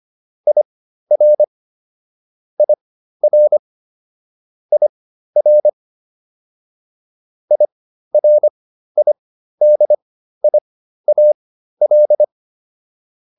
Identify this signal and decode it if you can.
Morse code, heard as a keyed tone, about 25 wpm, spaced as for 13 wpm, 600 Hz. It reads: IR IR IR IRIDIAL